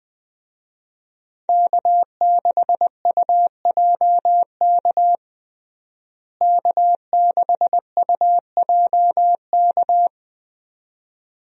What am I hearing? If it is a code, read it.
Morse code, heard as a keyed tone, about 20 wpm, 700 Hz. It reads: K6UJK K6UJK
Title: USA Call Signs - 5 Characters - Repeated 2x - 20wpm